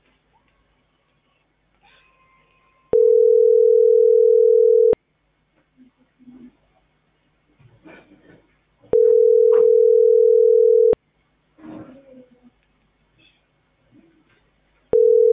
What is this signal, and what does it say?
Ringback tone (North America), 440 + 480 Hz.